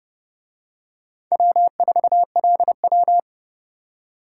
Morse code, heard as a keyed tone, about 29 wpm, 700 Hz. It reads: W4LW